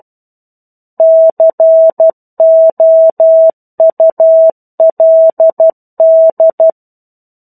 Morse code, heard as a keyed tone, about 12 wpm, 650 Hz. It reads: COULD
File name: 01-Chapter-01 12 WPM